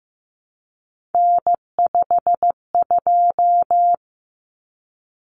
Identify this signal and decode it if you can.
Morse code, heard as a keyed tone, about 15 wpm, 700 Hz. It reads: N52
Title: Common Cut Numbers - 3-Digit 15wpm